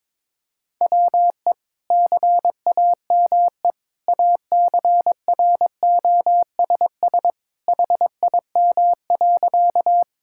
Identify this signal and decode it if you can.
Morse code, heard as a keyed tone, about 22 wpm, 700 Hz. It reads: WE CAME ACROSS HIM.
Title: Sentences from Top 1000 Words - Review Part 5 - 22wpm